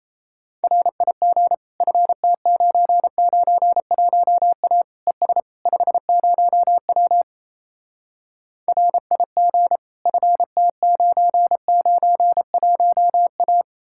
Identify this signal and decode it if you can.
Morse code, heard as a keyed tone, about 37 wpm, 700 Hz. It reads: RIG FT991A ES 50W RIG FT991A